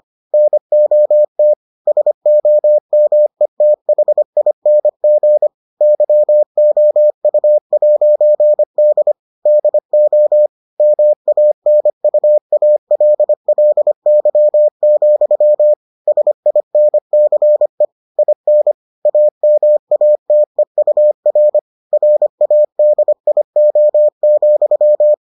Morse code, heard as a keyed tone, about 25 wpm, 600 Hz. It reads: NOT SOMETHING YOU'D DO MANUALLY, SINCE IN AMATEUR RADIO,